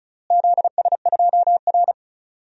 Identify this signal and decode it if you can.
Morse code, heard as a keyed tone, about 36 wpm, 700 Hz. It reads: ZS2R
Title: DXCC-100 Call Signs - 4 Characters - Repeated 3x - 35wpm